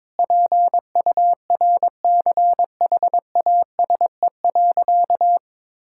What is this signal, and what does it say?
Morse code, heard as a keyed tone, about 22 wpm, 700 Hz. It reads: PURCHASE.
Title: Sentences from Top 1000 Words - Review Part 5 - 22wpm